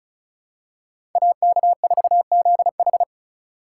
Morse code, heard as a keyed tone, about 36 wpm, 700 Hz. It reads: AK4ZH
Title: USA Call Signs - 3x Character Spacing - 35wpm